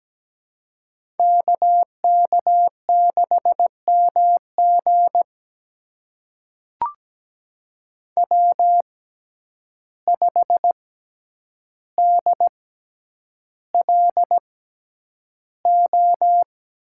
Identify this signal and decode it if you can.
Morse code, heard as a keyed tone, about 17 wpm, spaced as for 9 wpm, 700 Hz. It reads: KK6MG W5DLO
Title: USA Call Signs - 5 Characters - 6x Character Spacing - 17wpm